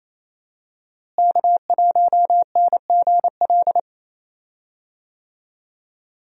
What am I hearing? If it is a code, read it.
Morse code, heard as a keyed tone, about 28 wpm, 700 Hz. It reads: K1NGL
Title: USA Call Signs - 5 Characters - Repeated 3x - 28wpm